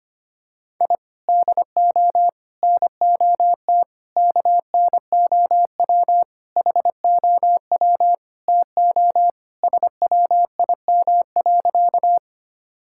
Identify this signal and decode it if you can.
Morse code, heard as a keyed tone, about 25 wpm, 700 Hz. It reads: I DO NOT KNOW HOW TO SWIM.
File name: practice / 25wpm / Sentences with 4-Word Phrases - 25wpm